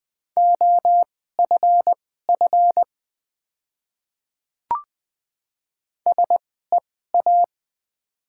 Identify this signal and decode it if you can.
Morse code, heard as a keyed tone, about 20 wpm, spaced as for 14 wpm, 700 Hz. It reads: OFF SEA
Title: 2x Character Spacing - Top 300 Words - 20wpm